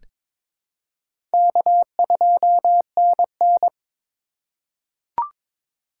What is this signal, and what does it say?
Morse code, heard as a keyed tone, about 22 wpm, 700 Hz. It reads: K2NN
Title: USA Call Signs - 4 Characters - 3x Character Spacing - 22wpm